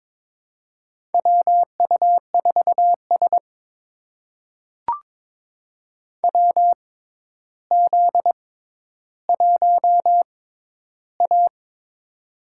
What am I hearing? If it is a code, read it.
Morse code, heard as a keyed tone, about 22 wpm, spaced as for 11 wpm, 700 Hz. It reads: WU4S WZ1A
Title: USA Call Signs - 4 Characters - 6x Character Spacing - 22wpm